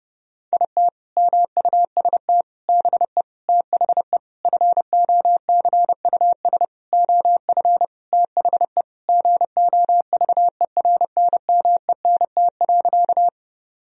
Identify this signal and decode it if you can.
Morse code, heard as a keyed tone, about 30 wpm, 700 Hz. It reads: IT MUST BE THE FOCUS OF THE GOVERNMENT.
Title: Sentences from Top 1000 Words - Review Part 4 - 30wpm